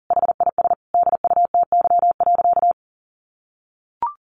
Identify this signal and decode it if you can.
Morse code, heard as a keyed tone, about 40 wpm, 700 Hz. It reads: HIS DUTY.
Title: Sentences from Top 1500 Words 40wpm